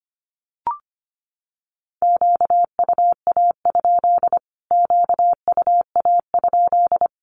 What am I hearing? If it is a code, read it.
Morse code, heard as a keyed tone, about 25 wpm, 700 Hz. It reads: QUA? QUA?
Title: Q-Codes Repeated 3x 25wpm